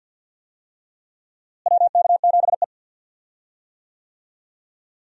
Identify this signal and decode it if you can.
Morse code, heard as a keyed tone, about 52 wpm, 700 Hz. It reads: WK7E
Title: USA Call Signs - 4 Characters - Repeated 3x - 50wpm